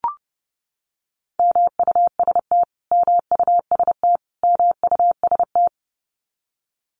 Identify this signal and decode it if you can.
Morse code, heard as a keyed tone, about 29 wpm, 700 Hz. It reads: MUST MUST MUST
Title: Top 200 Words Repeated 30wpm